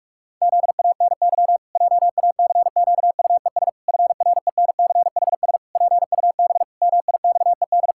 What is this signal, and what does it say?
Morse code, heard as a keyed tone, about 46 wpm, 700 Hz. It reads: ZANY JACQUES FRENCHS PUB MIXED
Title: Pangram Challenge v2 - 45wpm